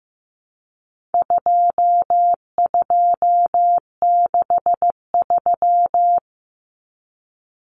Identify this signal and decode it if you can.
Morse code, heard as a keyed tone, about 15 wpm, 700 Hz. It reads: 2263
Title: Common Cut Numbers - 4-Digit 15wpm